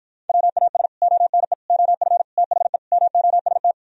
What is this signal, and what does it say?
Morse code, heard as a keyed tone, about 53 wpm, 700 Hz. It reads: WAS ONE OF THE MOST